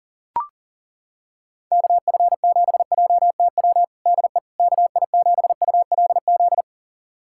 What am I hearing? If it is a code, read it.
Morse code, heard as a keyed tone, about 40 wpm, 700 Hz. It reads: KF7JTW DE KI7ULZ
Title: USA Callsign DE Callsign 40wpm